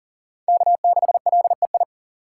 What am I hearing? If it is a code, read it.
Morse code, heard as a keyed tone, about 38 wpm, 700 Hz. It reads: K6LEI
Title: USA Call Signs - 5 Characters - 2x Character Spacing - 40wpm